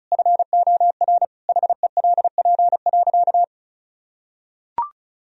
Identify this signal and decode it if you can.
Morse code, heard as a keyed tone, about 36 wpm, 700 Hz. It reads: FOR HELP.
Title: Sentences from Top 1000 Words - Review Part 4 - 35wpm